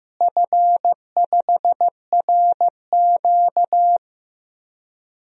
Morse code, heard as a keyed tone, about 15 wpm, 700 Hz. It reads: F5RQ
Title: DXCC-100 Call Signs - 4 Characters - Repeated 2x - 15wpm